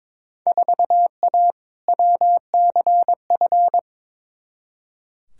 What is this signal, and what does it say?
Morse code, heard as a keyed tone, about 21 wpm, 700 Hz. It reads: 4A WCF
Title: Field Day Exchanges - 22wpm